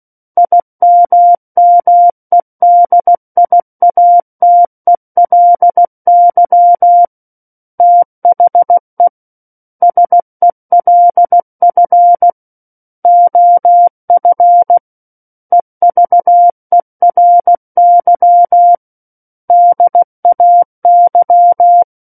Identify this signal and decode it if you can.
Morse code, heard as a keyed tone, about 16 wpm, 700 Hz. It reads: IMMEDIATELY THE SELF OF EVERY DAY
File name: Book One / 07-Chapter 7 - 16 WPM